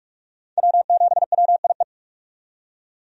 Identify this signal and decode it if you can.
Morse code, heard as a keyed tone, about 41 wpm, 700 Hz. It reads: W7WIE
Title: USA Call Signs - 5 Characters - 45wpm